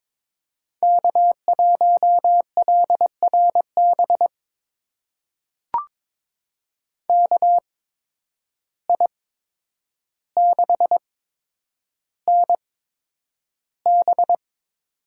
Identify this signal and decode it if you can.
Morse code, heard as a keyed tone, about 21 wpm, spaced as for 9 wpm, 700 Hz. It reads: K1LRB KI6NB
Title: USA Call Signs - 5 Characters - 8x Character Spacing - 22wpm